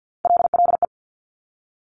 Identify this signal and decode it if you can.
Morse code, heard as a keyed tone, about 50 wpm, 700 Hz. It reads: LLE
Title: Top Three Letter Combinations in Words 50wpm